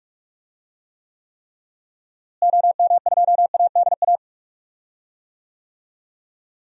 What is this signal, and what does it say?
Morse code, heard as a keyed tone, about 45 wpm, 700 Hz. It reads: OM2ADA
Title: DXCC-100 Call Signs - 6 Characters - Repeated 3x - 45wpm